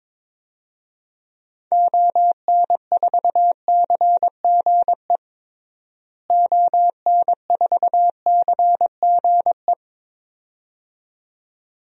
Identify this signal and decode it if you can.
Morse code, heard as a keyed tone, about 22 wpm, 700 Hz. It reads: ON4CGE ON4CGE
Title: DXCC-100 Call Signs - 6 Characters - Repeated 2x - 22wpm